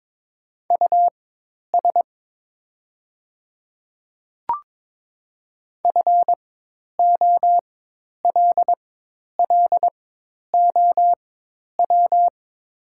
Morse code, heard as a keyed tone, about 22 wpm, spaced as for 10 wpm, 700 Hz. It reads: US FOLLOW
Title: 4x Character Spacing - Top 200 Words - 22wpm